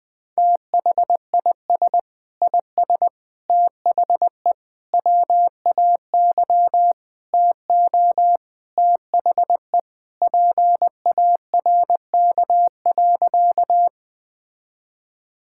Morse code, heard as a keyed tone, about 20 wpm, 700 Hz. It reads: THIS IS THE WAY TO THE PARK.
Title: Sentences with 4-Word Phrases - Spoken Beginning - 20wpm